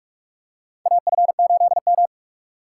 Morse code, heard as a keyed tone, about 46 wpm, 700 Hz. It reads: AF8M
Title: USA Call Signs - 4 Characters - 2x Character Spacing - 45wpm